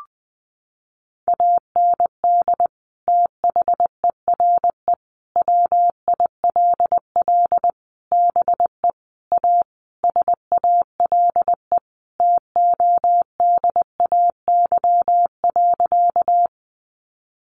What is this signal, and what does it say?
Morse code, heard as a keyed tone, about 20 wpm, 700 Hz. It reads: AND THERE WILL BE A SALE TODAY.
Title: Sentences from Top 1400 Words 20wpm